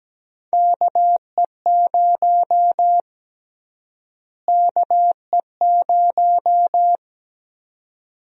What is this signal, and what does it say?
Morse code, heard as a keyed tone, about 17 wpm, 700 Hz. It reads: KE0 KE0
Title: USA Call Sign Prefixes - Repeated 2x - 17wpm